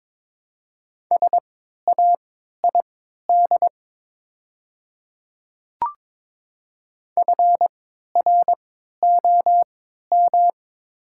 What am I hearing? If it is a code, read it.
Morse code, heard as a keyed tone, about 22 wpm, spaced as for 12 wpm, 700 Hz. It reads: SAID FROM
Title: 3x Character Spacing - Top 100 Words - 22wpm